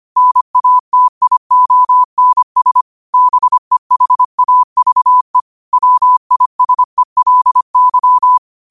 Morse code, heard as a keyed tone, about 25 wpm, 1000 Hz. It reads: NATIONS BEHAVE WISELY
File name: qotdfiles / 25wpm